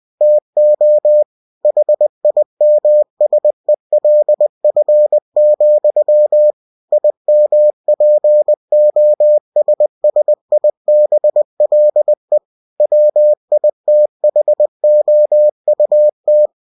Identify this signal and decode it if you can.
Morse code, heard as a keyed tone, about 20 wpm, 600 Hz. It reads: TO HIMSELF, IMPOSSIBLE WITHOUT
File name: cw-books / winnie-the-pooh / winnie-20wpm-page-0063